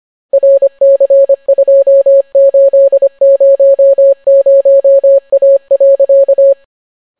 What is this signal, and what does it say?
Morse code, heard as a keyed tone, about 25 wpm, 550 Hz. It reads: RC2800A.